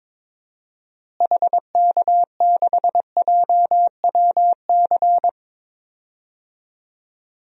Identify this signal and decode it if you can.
Morse code, heard as a keyed tone, about 21 wpm, 700 Hz. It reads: HK6JWC
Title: DXCC-100 Call Signs - 6 Characters - Repeated 3x - 22wpm